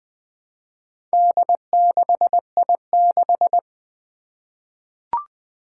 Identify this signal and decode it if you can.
Morse code, heard as a keyed tone, about 20 wpm, 700 Hz. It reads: D6I6
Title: Four Challenging Letter-Number Combinations 20wpm